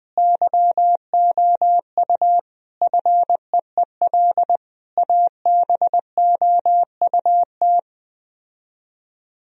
Morse code, heard as a keyed tone, about 20 wpm, 700 Hz. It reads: YOU FEEL ABOUT